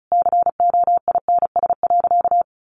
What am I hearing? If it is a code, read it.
Morse code, heard as a keyed tone, about 34 wpm, 700 Hz. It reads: COINS.
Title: A Sound Of Thunder 35wpm